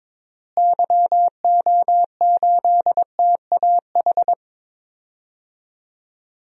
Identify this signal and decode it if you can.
Morse code, heard as a keyed tone, about 22 wpm, 700 Hz. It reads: YO8TAH
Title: DXCC-100 Call Signs - 6 Characters - Repeated 2x - 22wpm